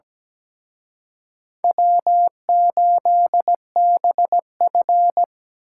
Morse code, heard as a keyed tone, about 20 wpm, 700 Hz. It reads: W8BF
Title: Speed Racing - USA 1x2 Call Signs 17wpm